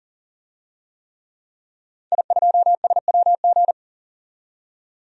Morse code, heard as a keyed tone, about 40 wpm, 700 Hz. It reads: I2SWG
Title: DXCC-100 Call Signs - 5 Characters - 40wpm